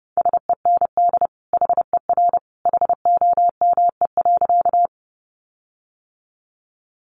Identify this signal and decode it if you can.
Morse code, heard as a keyed tone, about 28 wpm, 700 Hz. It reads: SEND HER HOME.
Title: Sentences from Top 1500 Words 30wpm